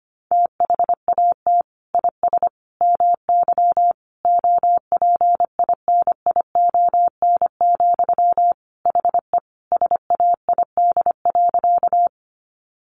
Morse code, heard as a keyed tone, about 25 wpm, 700 Hz. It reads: THAT IS MY OPINION, HE SAID.
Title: Sentences from Top 1200 Words 25wpm